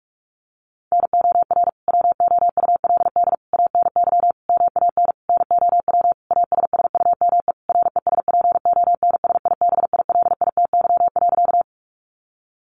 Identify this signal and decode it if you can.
Morse code, heard as a keyed tone, about 44 wpm, 700 Hz. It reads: NOR WOULD ANY MAN NOW ASSUME RESPONSIBILITY.
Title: Sentences from Top 1500 Words 45wpm